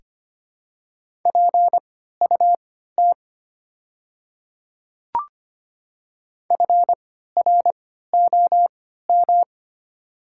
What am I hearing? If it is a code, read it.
Morse code, heard as a keyed tone, about 25 wpm, spaced as for 14 wpm, 700 Hz. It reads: PUT FROM
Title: 3x Character Spacing - Top 500 Words - 25wpm